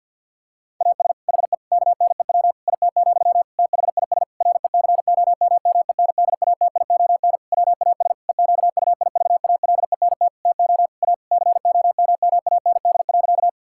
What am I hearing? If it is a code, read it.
Morse code, heard as a keyed tone, about 50 wpm, 700 Hz. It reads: AS HE KNEW IT, THIS RECOMMENDATION WAS EQUIVALENT TO A COMMAND.
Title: Sentences from Top 1600 Words 50wpm